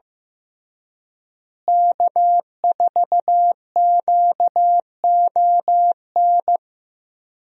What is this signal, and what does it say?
Morse code, heard as a keyed tone, about 15 wpm, 700 Hz. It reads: K4QON
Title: USA Call Signs - 5 Characters - Repeated 2x - 15wpm